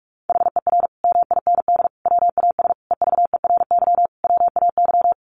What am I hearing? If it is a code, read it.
Morse code, heard as a keyed tone, about 44 wpm, 700 Hz. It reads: HER MIND WAS EVERY WAY
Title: Sentences from Top 1400 Words 45wpm